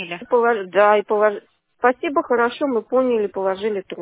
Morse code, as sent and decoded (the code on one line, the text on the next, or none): none